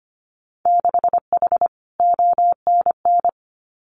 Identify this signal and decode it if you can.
Morse code, heard as a keyed tone, about 25 wpm, 700 Hz. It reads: 6H ONN
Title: Winter Field Day Exchanges Encore 25wpm